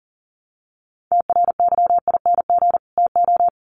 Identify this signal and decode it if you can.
Morse code, heard as a keyed tone, about 40 wpm, 700 Hz. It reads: TRYING TO
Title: Sets of 2 Words 40wpm